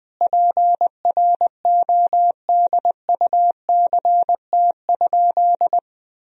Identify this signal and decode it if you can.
Morse code, heard as a keyed tone, about 20 wpm, 700 Hz. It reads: PRODUCT?
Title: Sentences from Top 1000 Words - Review Part 3 - 20wpm